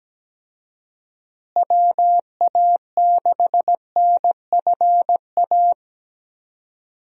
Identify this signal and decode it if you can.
Morse code, heard as a keyed tone, about 17 wpm, 700 Hz. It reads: WA6NFA
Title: USA Call Signs - 6 Characters - Repeated 2x - 17wpm